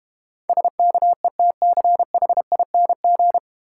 Morse code, heard as a keyed tone, about 30 wpm, 700 Hz. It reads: SKETCHING
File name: Easy Words - Speed-Racing with Hints - 25wpm